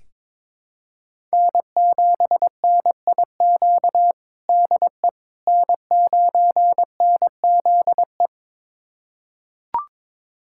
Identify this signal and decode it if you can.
Morse code, heard as a keyed tone, about 22 wpm, 700 Hz. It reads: N7NIQ DE N9NZE
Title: USA Callsign DE Callsign - 5 Characters - 22wpm